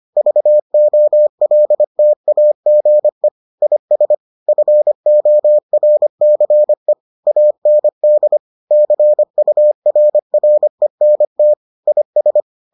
Morse code, heard as a keyed tone, about 25 wpm, 600 Hz. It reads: VOLTAGE IS FORCE AND CURRENT IS